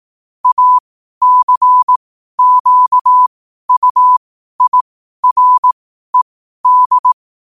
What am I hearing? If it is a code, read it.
Morse code, heard as a keyed tone, about 18 wpm, spaced as for 12 wpm, 1000 Hz. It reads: ACQUIRED
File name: qotdfiles / 12wpm